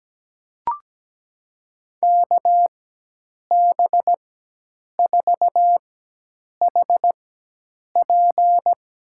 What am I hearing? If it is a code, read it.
Morse code, heard as a keyed tone, about 17 wpm, spaced as for 8 wpm, 700 Hz. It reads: KB4HP
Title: USA Call Signs - 4x Character Spacing - 17wpm